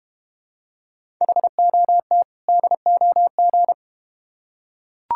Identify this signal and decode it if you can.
Morse code, heard as a keyed tone, about 32 wpm, 700 Hz. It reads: HOT DOG